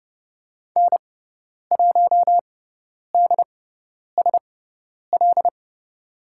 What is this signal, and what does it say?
Morse code, heard as a keyed tone, about 30 wpm, spaced as for 10 wpm, 700 Hz. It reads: N1DSL
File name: Lesson 16g - Letter c - Call Signs 30wpm